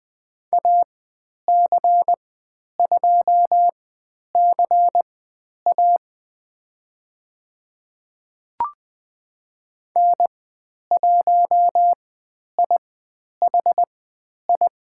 Morse code, heard as a keyed tone, about 20 wpm, spaced as for 10 wpm, 700 Hz. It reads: AC2CA N1IHI
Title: 102 Lesson 17g - Letter u - Call Signs 20wpm